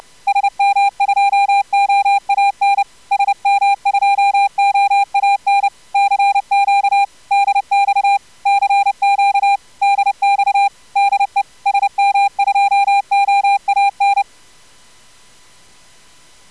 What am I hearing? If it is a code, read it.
Morse code, heard as a keyed tone, about 30 wpm, 800 Hz. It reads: SM2OAN SM2OAN CQ DX CQ DX DE SM2OAN
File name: CW jitter with corr-100-10-50